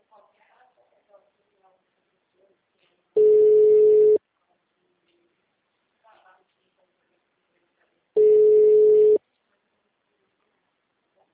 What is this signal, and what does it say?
Ringback tone (Europe, 425 Hz), with speech.